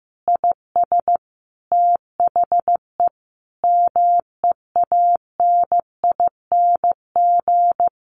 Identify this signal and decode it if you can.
Morse code, heard as a keyed tone, about 15 wpm, 700 Hz. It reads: IS THE MEANING
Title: Sentences from Top 900 Words 15wpm